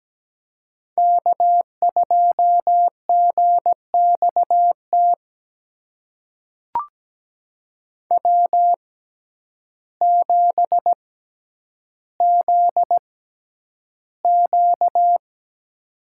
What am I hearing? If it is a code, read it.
Morse code, heard as a keyed tone, about 17 wpm, spaced as for 9 wpm, 700 Hz. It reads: K2GXT W7ZQ
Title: USA Call Signs - 6x Character Spacing - 17wpm